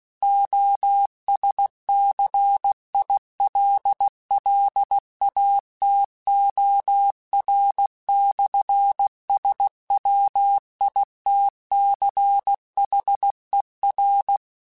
Morse code, heard as a keyed tone, about 16 wpm, 800 Hz. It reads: OSCILLATOR/SWITCHER
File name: Technical Subjects / Technical Subjects-mp3s / Sw Pwr Sup @15